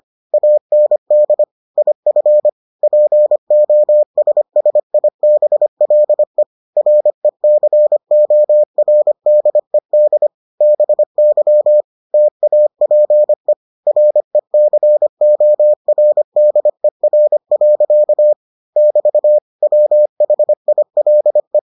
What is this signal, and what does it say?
Morse code, heard as a keyed tone, about 25 wpm, 600 Hz. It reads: AND IF POSSIBLE RECORDED BY TAPE RECORDER. <BT> WHILE